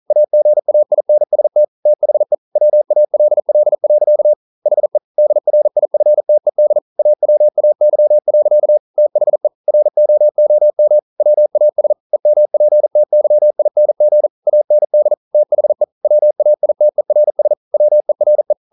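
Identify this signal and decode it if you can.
Morse code, heard as a keyed tone, about 40 wpm, 600 Hz. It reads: AGAINST THE WALL. HE DRIFTED AWAY. THE ROOM WAS EMPTYING AND THE WAITERS WERE
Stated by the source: Morse code practice at 41 wpm, from CW med Gnister.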